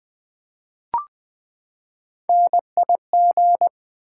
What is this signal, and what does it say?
Morse code, heard as a keyed tone, about 20 wpm, 700 Hz. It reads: NIG
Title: Build A Word - 3 Letters At A Time - 9-Letter Words - 20wpm